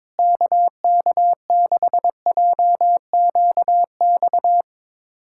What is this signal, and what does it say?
Morse code, heard as a keyed tone, about 22 wpm, 700 Hz. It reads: KK6JQX
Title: USA Callsign DE Callsign - 6 Characters - 22wpm